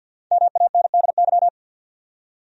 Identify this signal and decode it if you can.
Morse code, heard as a keyed tone, about 52 wpm, 700 Hz. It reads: MANDY
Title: USA Common Names - 50wpm